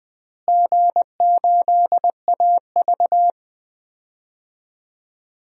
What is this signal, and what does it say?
Morse code, heard as a keyed tone, about 20 wpm, 700 Hz. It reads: G8AV
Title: DXCC-100 Call Signs - 4 Characters - Repeated 2x - 20wpm